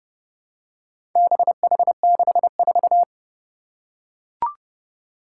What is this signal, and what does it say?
Morse code, heard as a keyed tone, about 30 wpm, 700 Hz. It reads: BH64
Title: Four Challenging Letter-Number Combinations 30wpm